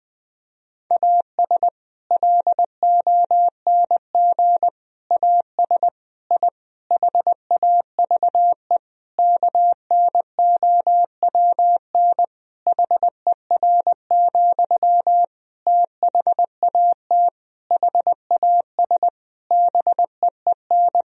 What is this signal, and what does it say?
Morse code, heard as a keyed tone, about 21 wpm, 700 Hz. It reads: AS LONG AS I HAVE KNOWN HER, THAT HAS BEEN
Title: Sentences from Top 1000 Words - Review Part 3 - 20wpm